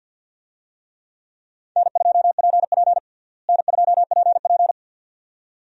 Morse code, heard as a keyed tone, about 50 wpm, 700 Hz. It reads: N2PP N2PP
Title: USA Call Signs - 4 Characters - Repeated 2x - 50wpm